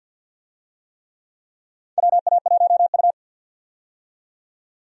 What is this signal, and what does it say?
Morse code, heard as a keyed tone, about 48 wpm, 700 Hz. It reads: WA1U
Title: USA Call Signs - 4 Characters - 50wpm